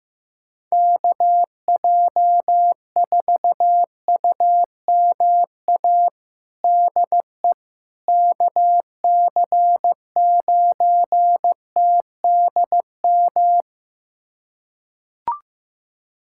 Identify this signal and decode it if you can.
Morse code, heard as a keyed tone, about 16 wpm, 700 Hz. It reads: KJ4UMA DE KC9TDM
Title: USA Callsign DE Callsign - 6 Characters - 15wpm